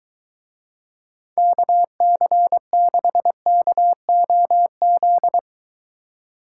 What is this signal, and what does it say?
Morse code, heard as a keyed tone, about 23 wpm, 700 Hz. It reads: KC6KOZ